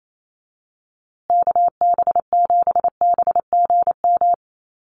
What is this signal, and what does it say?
Morse code, heard as a keyed tone, about 28 wpm, 700 Hz. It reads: KB7BGM
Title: USA Call Signs Repeated 3x 28wpm